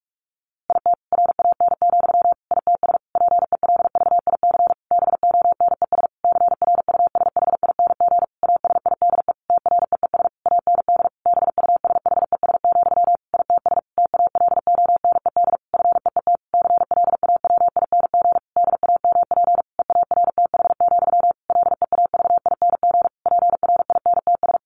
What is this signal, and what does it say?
Morse code, heard as a keyed tone, about 44 wpm, 700 Hz. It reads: IT RAN, ITS PELVIC BONES CRUSHING ASIDE TREES AND BUSHES, ITS TALONED FEET CLAWING DAMP EARTH, LEAVING PRINTS
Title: A Sound Of Thunder 45wpm